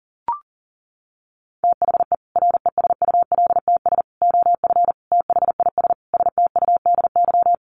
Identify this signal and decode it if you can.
Morse code, heard as a keyed tone, about 40 wpm, 700 Hz. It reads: THE RESULTS OF THIS STUDY
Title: Sets of 5 Words 40wpm